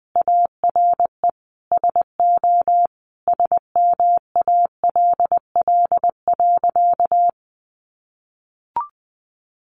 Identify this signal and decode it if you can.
Morse code, heard as a keyed tone, about 20 wpm, 700 Hz. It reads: ARE SO SMALL.
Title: Additional Easy Phrases 20wpm